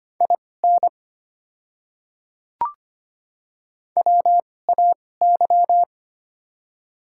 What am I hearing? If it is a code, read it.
Morse code, heard as a keyed tone, about 25 wpm, spaced as for 18 wpm, 700 Hz. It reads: IN WAY
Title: 2x Character Spacing - Top 100 Words - 25wpm